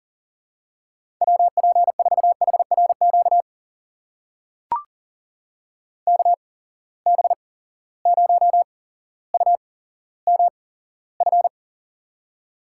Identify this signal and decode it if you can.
Morse code, heard as a keyed tone, about 40 wpm, spaced as for 17 wpm, 700 Hz. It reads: WP4HRQ KB0UMF
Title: USA Call Signs - 6 Characters - 8x Character Spacing - 40wpm